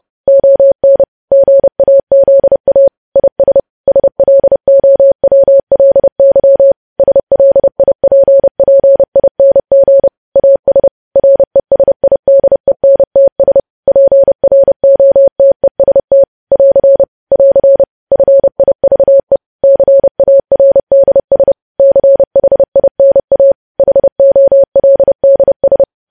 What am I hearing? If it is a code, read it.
Morse code, heard as a keyed tone, about 30 wpm, 550 Hz. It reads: ON GAZA IS SLOWLY SLIPPING AS RESIDENTS PROTEST <AR> <AR> FIVE CARDS CHINA HOLDS
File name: podcast / News headlines-1x-30x30